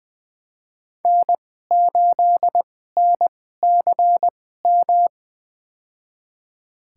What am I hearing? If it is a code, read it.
Morse code, heard as a keyed tone, about 20 wpm, spaced as for 14 wpm, 700 Hz. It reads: N8NCM
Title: USA Call Signs - 2x Character Spacing - 20wpm